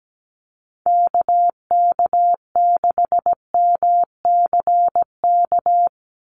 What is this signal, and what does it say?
Morse code, heard as a keyed tone, about 17 wpm, 700 Hz. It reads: KK6MCK
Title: USA Call Signs 17wpm